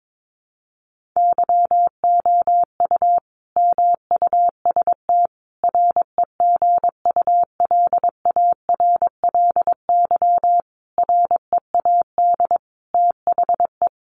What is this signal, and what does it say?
Morse code, heard as a keyed tone, about 22 wpm, 700 Hz. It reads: YOU MUST REGULARLY READ THE
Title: Sentences from Top 1400 Words 22wpm